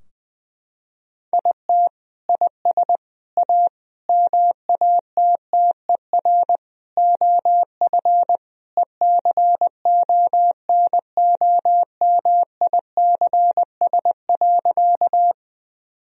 Morse code, heard as a keyed tone, about 21 wpm, 700 Hz. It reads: IT IS A MATTER OF ECONOMICS.
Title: Sentences from Top 1000 Words - Review Part 3 - 20wpm